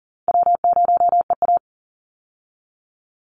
Morse code, heard as a keyed tone, about 40 wpm, 700 Hz. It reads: W0EA
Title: Easy Call Signs - 40wpm